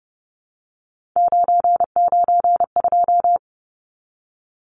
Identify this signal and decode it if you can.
Morse code, heard as a keyed tone, about 30 wpm, 700 Hz. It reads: 992
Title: Numbers - 3-Digit 30wpm